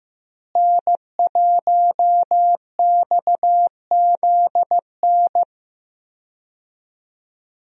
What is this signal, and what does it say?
Morse code, heard as a keyed tone, about 15 wpm, 700 Hz. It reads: N1XZN
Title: USA Call Signs - 5 Characters - Repeated 2x - 15wpm